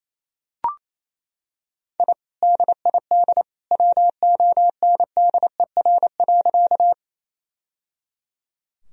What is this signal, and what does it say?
Morse code, heard as a keyed tone, about 28 wpm, 700 Hz. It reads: I DID WONDER.
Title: Sentences from Top 1000 Words - Review Part 3 - 28wpm